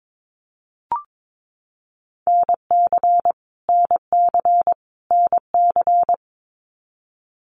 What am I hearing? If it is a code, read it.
Morse code, heard as a keyed tone, about 22 wpm, 700 Hz. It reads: NC NC NC
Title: US States Repeated 3x 22wpm